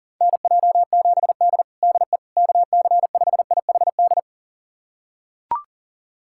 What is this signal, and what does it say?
Morse code, heard as a keyed tone, about 42 wpm, 700 Hz. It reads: NJ7D DE KC5IHD
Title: USA Callsign DE Callsign 40wpm